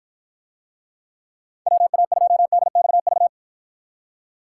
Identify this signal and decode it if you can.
Morse code, heard as a keyed tone, about 53 wpm, 700 Hz. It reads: WA2DXV